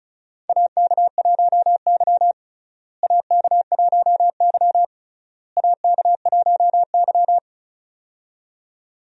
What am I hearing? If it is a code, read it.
Morse code, heard as a keyed tone, about 36 wpm, 700 Hz. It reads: AK1Y AK1Y AK1Y
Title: USA Call Signs - 4 Characters - Repeated 3x - 35wpm